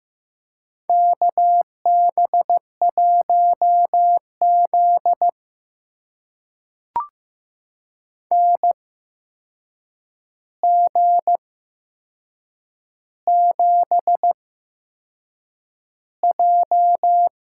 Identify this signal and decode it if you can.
Morse code, heard as a keyed tone, about 15 wpm, spaced as for 6 wpm, 700 Hz. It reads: KB1Z NG7J
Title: USA Call Signs - 8x Character Spacing - 15wpm